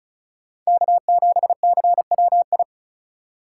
Morse code, heard as a keyed tone, about 43 wpm, 700 Hz. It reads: K7CWI
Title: Speed Racing - USA 1x3 Call Signs 35wpm